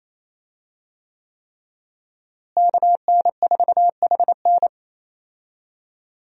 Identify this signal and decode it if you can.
Morse code, heard as a keyed tone, about 28 wpm, 700 Hz. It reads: KN4HN
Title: USA Call Signs - 5 Characters - Repeated 3x - 28wpm